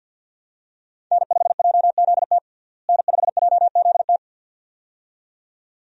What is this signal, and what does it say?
Morse code, heard as a keyed tone, about 49 wpm, 700 Hz. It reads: N5JZT N5JZT
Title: USA Call Signs - 5 Characters - Repeated 2x - 50wpm